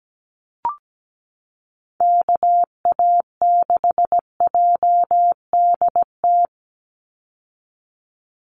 Morse code, heard as a keyed tone, about 17 wpm, 700 Hz. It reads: KA6JDT
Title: USA Call Signs 17wpm